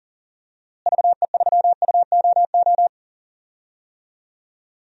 Morse code, heard as a keyed tone, about 40 wpm, 700 Hz. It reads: VE3UOO
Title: DXCC-100 Call Signs - 6 Characters - Repeated 3x - 40wpm